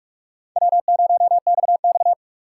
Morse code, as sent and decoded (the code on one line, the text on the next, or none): .-- ----- -..- -..-
W0XX